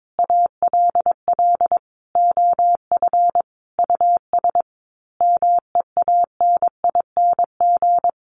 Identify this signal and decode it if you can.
Morse code, heard as a keyed tone, about 22 wpm, 700 Hz. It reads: ALL OF US MEANING
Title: A Sound Of Thunder 22wpm